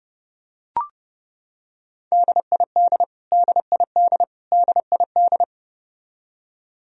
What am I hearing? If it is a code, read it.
Morse code, heard as a keyed tone, about 30 wpm, 700 Hz. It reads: DID DID DID
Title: Top 100 Words Repeated 30wpm